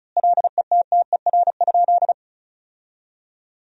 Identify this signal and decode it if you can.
Morse code, heard as a keyed tone, about 34 wpm, 700 Hz. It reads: LETTER?
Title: 2x Word-Spacing - Sentences from Top 300 Words - 35wpm